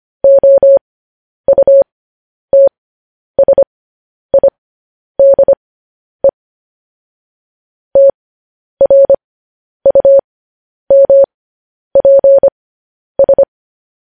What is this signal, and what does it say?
Morse code, heard as a keyed tone, about 25 wpm, spaced as for 10 wpm, 550 Hz. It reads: OUTSIDE TRUMPS
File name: podcast / News headlines-1x-25x10